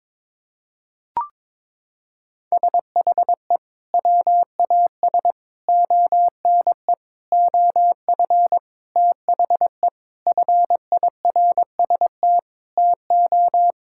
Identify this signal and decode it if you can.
Morse code, heard as a keyed tone, about 22 wpm, 700 Hz. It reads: SHE WAS ONE OF THE FIRST TO
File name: practice / 22wpm / Sentences with 4-Word Phrases - 22wpm